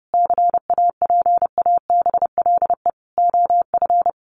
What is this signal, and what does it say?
Morse code, heard as a keyed tone, about 30 wpm, 700 Hz. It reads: CAPABLE OF
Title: Sentences from Top 1200 Words 30wpm